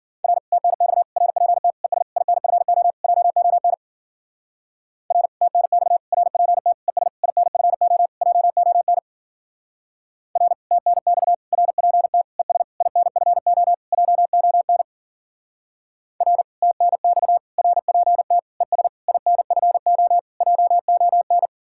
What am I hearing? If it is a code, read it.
Morse code, heard as a keyed tone, about 48 wpm, 700 Hz. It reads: R TNX RPT ES INFO JON R TNX RPT ES INFO JON R TNX RPT ES INFO JON R TNX RPT ES INFO JON
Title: Speed-Racing Common QSO Phrases - 40wpm